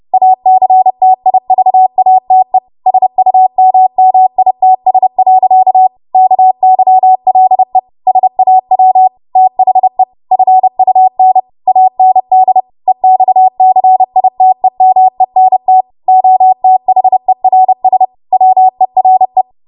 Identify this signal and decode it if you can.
Morse code, heard as a keyed tone, about 30 wpm, 750 Hz. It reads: ACTIVATE SUMMITS. KYLE SAW THE FUN AND EXCITEMENT OTHERS WERE
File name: Morse / 30WPM